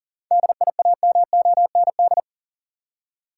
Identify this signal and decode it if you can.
Morse code, heard as a keyed tone, about 40 wpm, 700 Hz. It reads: DIAMOND
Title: Seven Letter Words - 40wpm